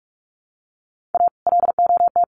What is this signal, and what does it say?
Morse code, heard as a keyed tone, about 44 wpm, 700 Hz. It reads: A LOT
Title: Sets of 5 Words 45wpm